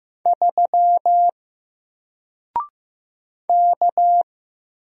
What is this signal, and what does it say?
Morse code, heard as a keyed tone, about 15 wpm, 700 Hz. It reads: 3 K
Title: Single Letter-Number Alt 15wpm